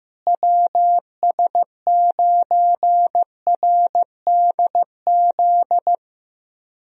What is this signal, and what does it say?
Morse code, heard as a keyed tone, about 15 wpm, 700 Hz. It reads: WS9RDZ
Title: USA Call Signs - 6 Characters - Repeated 2x - 15wpm